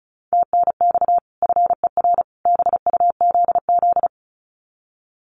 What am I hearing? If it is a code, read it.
Morse code, heard as a keyed tone, about 34 wpm, 700 Hz. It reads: TNX FER BUZZ
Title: Common QSO Phrases 35wpm